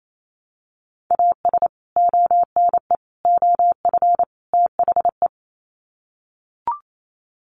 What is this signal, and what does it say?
Morse code, heard as a keyed tone, about 28 wpm, 700 Hz. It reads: AS ONE OF THE
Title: Sets of 4 Words 28wpm